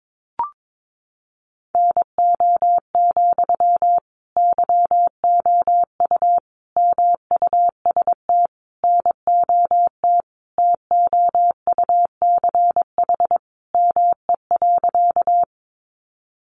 Morse code, heard as a keyed tone, about 22 wpm, 700 Hz. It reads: NO, YOU MUST NOT TOUCH ME.
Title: Sentences from Top 1200 Words 22wpm